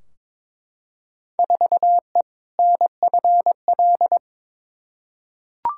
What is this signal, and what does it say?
Morse code, heard as a keyed tone, about 22 wpm, 700 Hz. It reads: 4E NFL
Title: Field Day Exchanges Encore - 22wpm